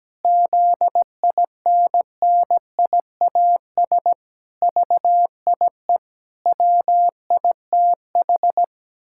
Morse code, heard as a keyed tone, about 17 wpm, 700 Hz. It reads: ZINNIAS VIE WITH
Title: Pangram Challenge v2 - 17wpm